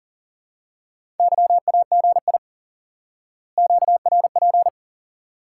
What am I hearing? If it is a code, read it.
Morse code, heard as a keyed tone, about 37 wpm, 700 Hz. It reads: YAGI QRP
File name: IQE Course 3 - Lesson 89 - YAGI - Two Elements - 40wpm